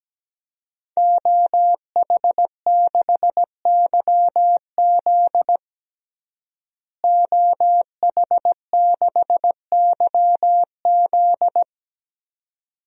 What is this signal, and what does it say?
Morse code, heard as a keyed tone, about 17 wpm, 700 Hz. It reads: OH6YZ OH6YZ
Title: DXCC-100 Call Signs - 5 Characters - Repeated 2x - 17wpm